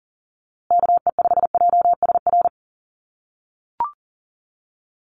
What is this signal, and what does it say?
Morse code, heard as a keyed tone, about 40 wpm, 700 Hz. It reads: KE5JSR
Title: USA Call Signs Repeated 3x 40wpm